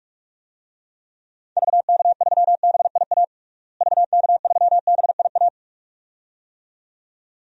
Morse code, heard as a keyed tone, about 46 wpm, 700 Hz. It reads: VK3BIA VK3BIA
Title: DXCC-100 Call Signs - 6 Characters - Repeated 2x - 45wpm